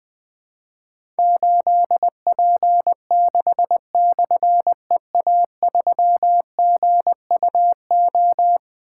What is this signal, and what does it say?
Morse code, heard as a keyed tone, about 20 wpm, 700 Hz. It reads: 8P6/EA3GUO
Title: Difficult Call Signs 20wpm